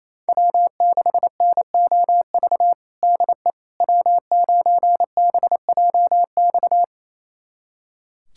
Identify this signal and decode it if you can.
Morse code, heard as a keyed tone, about 28 wpm, 700 Hz. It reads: W6NOV DE W9BJX
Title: USA Callsign DE Callsign - 5 Characters - 28wpm